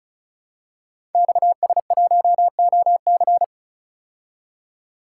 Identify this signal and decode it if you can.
Morse code, heard as a keyed tone, about 35 wpm, 700 Hz. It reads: XS1OC
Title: Code Groups - 5 Letter:Numbers - 35wpm